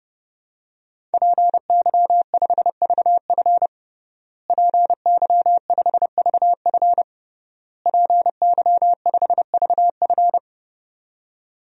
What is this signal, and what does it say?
Morse code, heard as a keyed tone, about 32 wpm, 700 Hz. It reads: PY5VF PY5VF PY5VF
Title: DXCC-100 Call Signs - 5 Characters - Repeated 3x - 30wpm